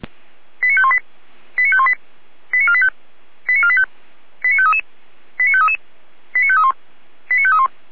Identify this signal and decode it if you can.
Five-tone selective call (ZVEI), heard as quick runs of five tones, tones of about 70 ms. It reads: 89518 89518 89485 89485 89533 89533 89531 89531